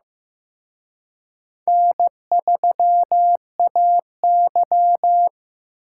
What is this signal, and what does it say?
Morse code, heard as a keyed tone, about 18 wpm, 700 Hz. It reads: N3AY
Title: Speed Racing - USA 1x2 Call Signs 15wpm